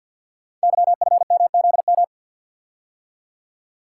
Morse code, heard as a keyed tone, about 50 wpm, 700 Hz. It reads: YFMZM
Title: Code Groups - 5 Letters - 50wpm